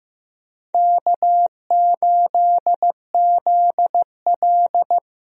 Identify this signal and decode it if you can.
Morse code, heard as a keyed tone, about 15 wpm, 700 Hz. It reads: K8ZL
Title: USA Call Signs - 4 Characters - 15wpm